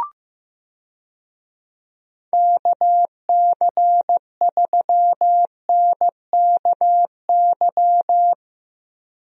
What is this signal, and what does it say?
Morse code, heard as a keyed tone, about 15 wpm, 700 Hz. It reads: KC3NKY
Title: USA Call Signs - 6 Characters - Repeated 3x - 15wpm